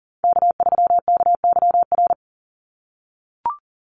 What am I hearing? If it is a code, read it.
Morse code, heard as a keyed tone, about 40 wpm, 700 Hz. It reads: K3KYR
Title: NA SOTA Chaser Callsigns 40wpm